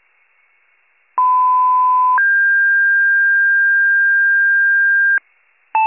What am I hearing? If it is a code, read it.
Two-tone page, heard as a one-second tone, then a three-second tone, about 1010 then 1600 Hz. It is heard from 1.2 to 5.2 s.
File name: pager / 94 RESCUE DUTY / wcalert